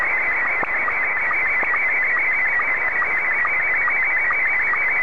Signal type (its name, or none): SSTV (slow-scan television)